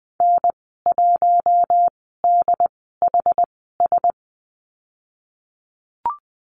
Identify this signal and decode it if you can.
Morse code, heard as a keyed tone, about 20 wpm, spaced as for 14 wpm, 700 Hz. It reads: N1DHS
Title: 088 Lesson 15g - Number 5 - Call Signs 20wpm